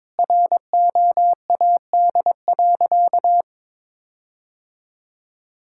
Morse code, heard as a keyed tone, about 22 wpm, 700 Hz. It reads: ROAD.
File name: practice / 22wpm / Sentences with 4-Word Phrases - 22wpm